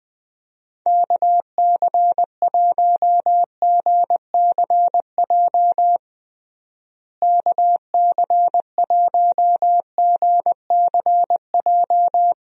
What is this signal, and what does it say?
Morse code, heard as a keyed tone, about 20 wpm, 700 Hz. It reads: KC1GCJ KC1GCJ
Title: USA Call Signs - 6 Characters - Repeated 3x - 20wpm